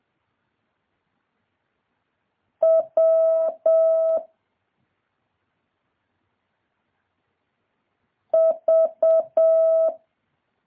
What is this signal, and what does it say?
Morse code, heard as a keyed tone, about 7 wpm, 650 Hz. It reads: W V
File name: jh8cbh / 2023cw